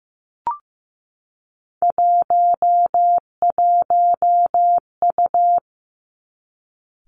Morse code, heard as a keyed tone, about 15 wpm, 700 Hz. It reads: 11U
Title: Three Challenging Letter-Number Combinations 15wpm